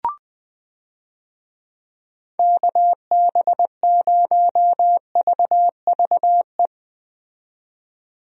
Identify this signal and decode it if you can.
Morse code, heard as a keyed tone, about 20 wpm, 700 Hz. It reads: KB0VVE
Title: USA Call Signs - 6 Characters - 20wpm